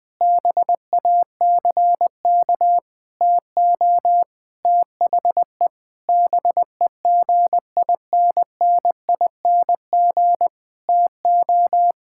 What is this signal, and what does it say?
Morse code, heard as a keyed tone, about 20 wpm, 700 Hz. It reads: BACK TO THE BEGINNING TO
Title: Sentences with 8-Word Phrases - Spoken Beginning - 20wpm